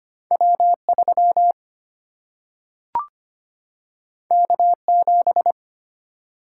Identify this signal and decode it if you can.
Morse code, heard as a keyed tone, about 25 wpm, 700 Hz. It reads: W3 K7
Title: Single-Letter Single-Number 25wpm